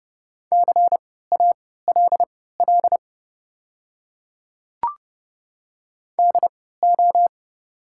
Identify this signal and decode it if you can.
Morse code, heard as a keyed tone, about 32 wpm, spaced as for 17 wpm, 700 Hz. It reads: CALL DO
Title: 3x Character Spacing - Top 100 Words - 30wpm